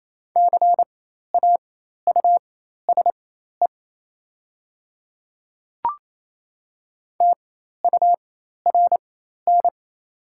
Morse code, heard as a keyed tone, about 27 wpm, spaced as for 13 wpm, 690 Hz. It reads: CAUSE TURN
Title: 4x Character Spacing - Top 200 Words - 28wpm